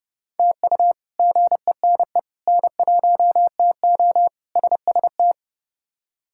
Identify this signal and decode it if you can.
Morse code, heard as a keyed tone, about 30 wpm, 700 Hz. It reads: TU GENE N1TO SST
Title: K1USN SST QSO Practice - 30wpm